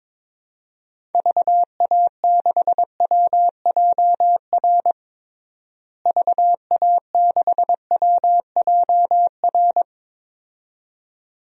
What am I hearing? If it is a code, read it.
Morse code, heard as a keyed tone, about 22 wpm, 700 Hz. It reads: VA6WJR VA6WJR
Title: DXCC-100 Call Signs - 6 Characters - Repeated 2x - 22wpm